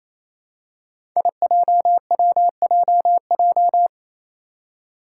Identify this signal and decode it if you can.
Morse code, heard as a keyed tone, about 28 wpm, 700 Hz. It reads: IJWJJ
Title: Five Challenging Letter-Number Combinations 28wpm